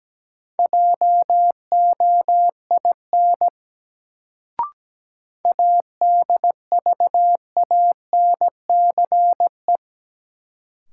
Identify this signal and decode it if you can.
Morse code, heard as a keyed tone, about 17 wpm, 700 Hz. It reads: JOIN ADVANCE
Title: Sentences from Top 1400 Words - New Word Drill Alt - 17wpm